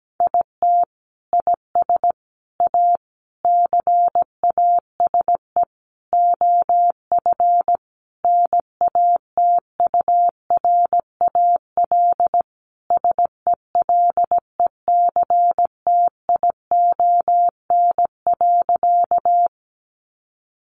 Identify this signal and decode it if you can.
Morse code, heard as a keyed tone, about 17 wpm, 700 Hz. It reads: IT IS A CASE OF NATURAL SELECTION.
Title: Sentences from Top 1400 Words 17wpm